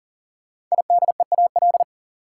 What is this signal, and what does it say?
Morse code, heard as a keyed tone, about 40 wpm, 700 Hz. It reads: IDEAL
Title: Sentences from Top 1400 Words - New Word Drill - 40wpm